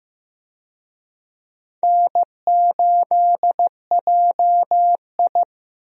Morse code, heard as a keyed tone, about 15 wpm, 700 Hz. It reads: N8JI
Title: USA Call Signs - 4 Characters - Repeated 3x - 15wpm